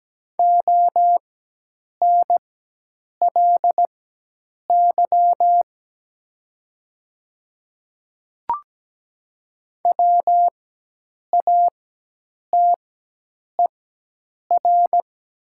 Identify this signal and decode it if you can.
Morse code, heard as a keyed tone, about 17 wpm, spaced as for 8 wpm, 700 Hz. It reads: ONLY WATER
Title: 4x Character Spacing - Top 200 Words - 17wpm